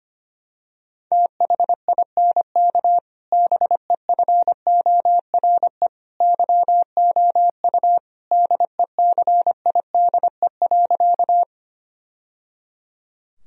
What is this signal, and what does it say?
Morse code, heard as a keyed tone, about 25 wpm, 700 Hz. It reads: THINK BEFORE YOU DECIDE.
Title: Sentences from Top 1000 Words - Review Part 3 - 25wpm